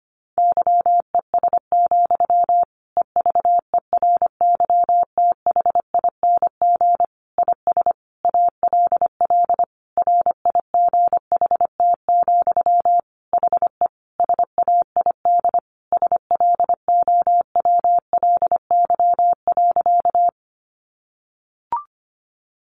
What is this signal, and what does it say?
Morse code, heard as a keyed tone, about 25 wpm, 700 Hz. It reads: YES, EVERYTHING IS ALL RIGHT, HE SAID SLOWLY.
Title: Sentences from Top 1100 Words 25wpm